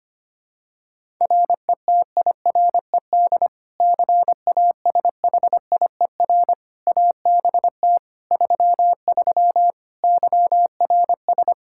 Morse code, heard as a keyed tone, about 25 wpm, 700 Hz. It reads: RETIRED CASHIER ABT 33 YRS
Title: Practice QSO - Exchange 3 - More Info - 25wpm